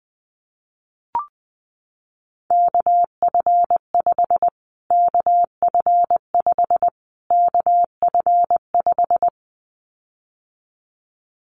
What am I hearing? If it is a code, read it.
Morse code, heard as a keyed tone, about 19 wpm, 700 Hz. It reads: KF5 KF5 KF5
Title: Callsign Prefixes for Top 25 DXCC - Repeated 3x - 20wpm